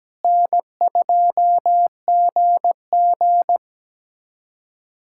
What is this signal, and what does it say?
Morse code, heard as a keyed tone, about 17 wpm, 700 Hz. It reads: N2GG
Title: USA Callsign DE Callsign - 4 Characters - 17wpm